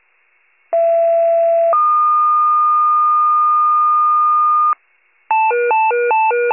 Two-tone page, heard as a one-second tone, then a three-second tone, about 670 then 1220 Hz. It is heard from 0.7 to 4.7 s.